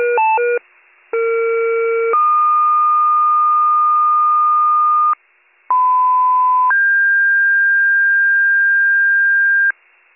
Two-tone page, heard as a one-second tone, then a three-second tone, about 460 then 1220 Hz, about 1010 then 1600 Hz, with noise. It is heard from 1.1 to 9.7 s.